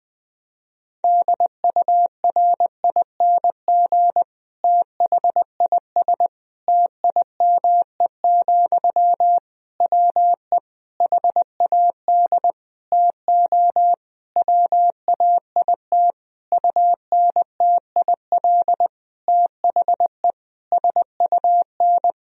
Morse code, heard as a keyed tone, about 20 wpm, 700 Hz. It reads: DURING THIS TIME, WE HAD TO WAIT UNTIL THE SUN
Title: Sentences from Top 1000 Words - Review Part 2 - 20wpm